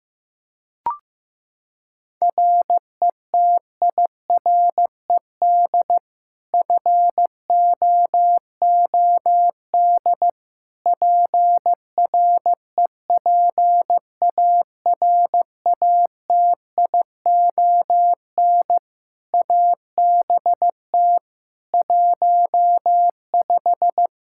Morse code, heard as a keyed tone, about 15 wpm, 700 Hz. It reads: RETIRED FOOD PREPARATION ABT 15
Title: K1USN SST QSO Practice 15wpm